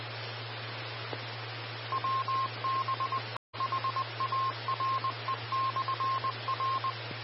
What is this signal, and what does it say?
Morse code, heard as a keyed tone, about 20 wpm, 1060 Hz. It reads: WB HARE/R
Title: Austin, TX Amateur Repeaters